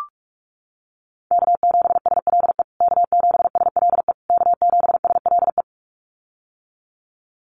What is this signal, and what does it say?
Morse code, heard as a keyed tone, about 44 wpm, 700 Hz. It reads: K7SLE K7SLE K7SLE
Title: USA Call Signs Repeated 3x 45wpm